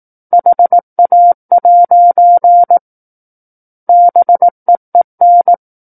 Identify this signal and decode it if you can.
Morse code, heard as a keyed tone, about 18 wpm, 700 Hz. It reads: HA' BEEN
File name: 18 WPM